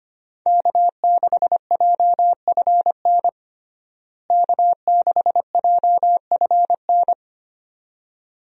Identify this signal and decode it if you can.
Morse code, heard as a keyed tone, about 25 wpm, 700 Hz. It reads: K6JFN K6JFN
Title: USA Call Signs - 5 Characters - Repeated 2x - 25wpm